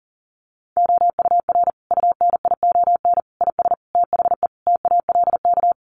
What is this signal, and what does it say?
Morse code, heard as a keyed tone, about 37 wpm, 700 Hz. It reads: OUR UNION IS THE TALK
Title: Sentences from Top 1400 Words 40wpm